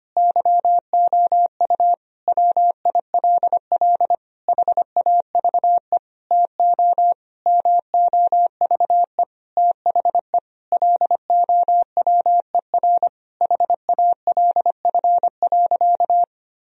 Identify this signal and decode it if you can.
Morse code, heard as a keyed tone, about 25 wpm, 700 Hz. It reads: YOU WILL HAVE TO MOVE THE LOWER HALF.
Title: Sentences from Top 1000 Words - Review Part 4 - 25wpm